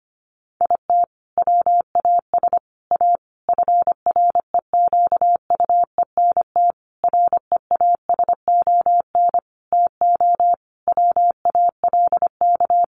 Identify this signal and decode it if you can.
Morse code, heard as a keyed tone, about 25 wpm, 700 Hz. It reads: IT WAS A FREQUENT REASON TO WALK
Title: Sentences from Top 1400 Words 25wpm